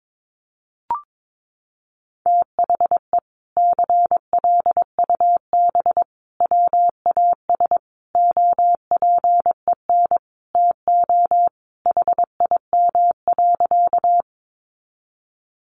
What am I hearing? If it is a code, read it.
Morse code, heard as a keyed tone, about 22 wpm, 700 Hz. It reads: THE CLUB WAS OPEN TO HIM.
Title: Sentences from Top 1200 Words 22wpm